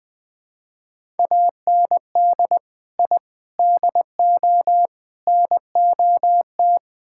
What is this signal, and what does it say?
Morse code, heard as a keyed tone, about 20 wpm, 700 Hz. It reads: AND I DO NOT
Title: Build A Phrase - 4-Word Phrases - 20wpm